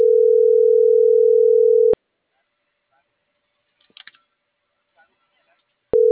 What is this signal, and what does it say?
Ringback tone (North America), 440 + 480 Hz.